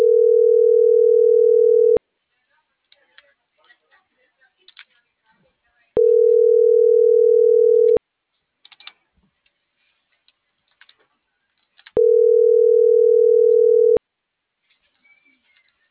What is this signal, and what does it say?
Ringback tone (North America), 440 + 480 Hz.
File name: CENTRAL NMP / out melanie.diaznmp - .45319